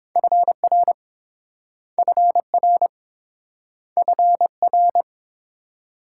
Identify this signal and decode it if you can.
Morse code, heard as a keyed tone, about 28 wpm, 690 Hz. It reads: FR FR FR